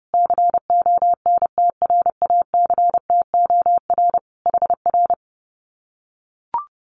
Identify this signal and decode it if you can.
Morse code, heard as a keyed tone, about 30 wpm, 700 Hz. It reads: CONTRACTOR HR
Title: Common QSO Phrases Encore 30wpm